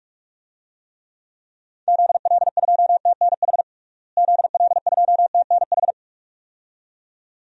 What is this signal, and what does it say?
Morse code, heard as a keyed tone, about 46 wpm, 700 Hz. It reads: ZL2TNH ZL2TNH
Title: DXCC-100 Call Signs - 6 Characters - Repeated 2x - 45wpm